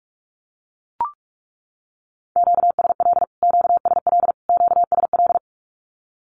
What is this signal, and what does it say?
Morse code, heard as a keyed tone, about 44 wpm, 700 Hz. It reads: QSL QSL QSL
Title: Q-Codes Repeated 3x 45wpm